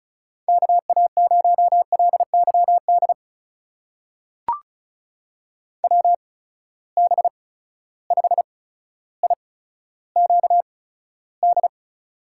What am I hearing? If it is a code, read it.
Morse code, heard as a keyed tone, about 36 wpm, spaced as for 15 wpm, 700 Hz. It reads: KA0LYD WB5IQD
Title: USA Call Signs - 6 Characters - 8x Character Spacing - 35wpm